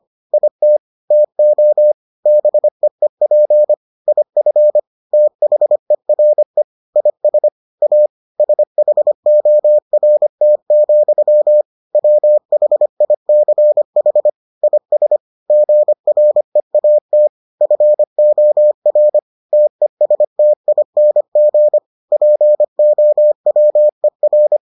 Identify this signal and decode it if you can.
Morse code, heard as a keyed tone, about 25 wpm, 600 Hz. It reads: IT TO BEEP IF THERE IS A SHORT, WHICH IS GREAT FOR TESTING POWER